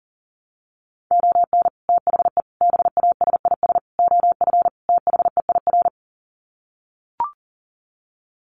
Morse code, heard as a keyed tone, about 40 wpm, 700 Hz. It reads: ON THE BASIS OF THEIR
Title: Sets of 5 Words Encore 40wpm